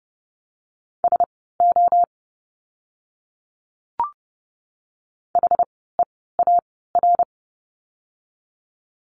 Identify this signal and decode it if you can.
Morse code, heard as a keyed tone, about 30 wpm, spaced as for 17 wpm, 700 Hz. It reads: SO HEAR
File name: course / 30:10wpm / 115 Lesson 19f - Letter w - Words 30wpm